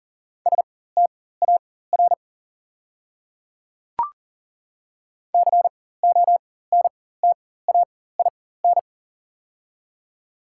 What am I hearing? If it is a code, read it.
Morse code, heard as a keyed tone, about 40 wpm, spaced as for 19 wpm, 690 Hz. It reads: STAR CONTAIN
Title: 4x Character Spacing - Top 500 Words - 40wpm